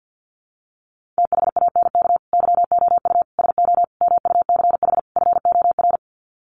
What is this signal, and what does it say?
Morse code, heard as a keyed tone, about 50 wpm, 700 Hz. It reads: THANK YOU SO MUCH FOR
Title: Sets of 5 Words 50wpm